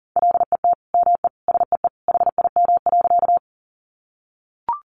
Morse code, heard as a keyed tone, about 40 wpm, 700 Hz. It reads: LET ME SEE HIM.
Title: Sentences from Top 300 Words 40wpm